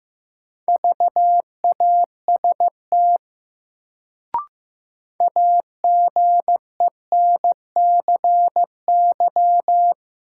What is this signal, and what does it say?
Morse code, heard as a keyed tone, about 15 wpm, 700 Hz. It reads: VAST AGENCY
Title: Sentences from Top 1300 Words - New Word Drill - 15wpm